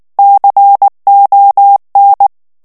Morse code, heard as a keyed tone, about 19 wpm, 800 Hz. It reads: CON
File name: morse mp3 / ESPAGNOL TEXTE EN CLAIR 2 VITESSE 18 MOTS MINUTES